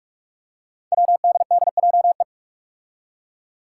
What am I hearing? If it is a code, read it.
Morse code, heard as a keyed tone, about 42 wpm, 700 Hz. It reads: WDDJE
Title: Five Challenging Letter-Number Combinations 45wpm